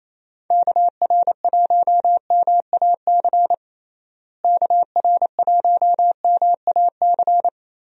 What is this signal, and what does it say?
Morse code, heard as a keyed tone, about 28 wpm, 700 Hz. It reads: KR1MAC KR1MAC
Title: USA Call Signs - 6 Characters - Repeated 3x - 28wpm